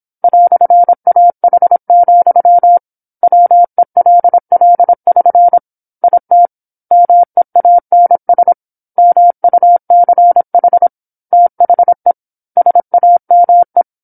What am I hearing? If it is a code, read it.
Morse code, heard as a keyed tone, about 27 wpm, 700 Hz. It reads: "AH, WELL<SN> IT MEANS MUCH THE SAME
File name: 08-Chapter-08 26 WPM